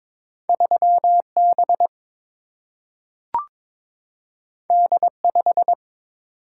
Morse code, heard as a keyed tone, about 22 wpm, 700 Hz. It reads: 3 B D 5
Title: Single-Letter Single-Number 22wpm